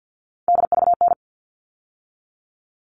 Morse code, heard as a keyed tone, about 48 wpm, 700 Hz. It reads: DVN
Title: All Cut Numbers - 3-Digit 50wpm